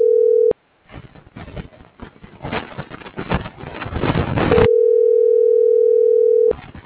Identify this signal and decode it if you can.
Ringback tone (North America), 440 + 480 Hz.